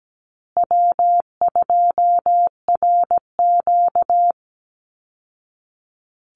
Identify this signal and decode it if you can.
Morse code, heard as a keyed tone, about 17 wpm, 700 Hz. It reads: W2RQ
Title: Call Signs Repeated 2x - 17wpm